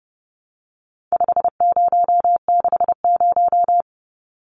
Morse code, heard as a keyed tone, about 30 wpm, 700 Hz. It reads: 5060
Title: Numbers - 4-Digit 30wpm